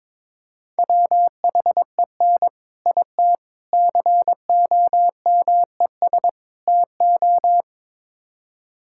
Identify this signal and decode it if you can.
Morse code, heard as a keyed tone, about 22 wpm, 700 Hz. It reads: WHEN IT COMES TO
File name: practice / 22wpm / Build A Phrase - 5-Word Phrases - 22wpm